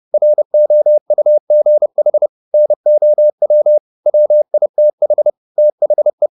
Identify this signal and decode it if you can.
Morse code, heard as a keyed tone, about 30 wpm, 600 Hz. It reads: ROUGH NOW WITH THE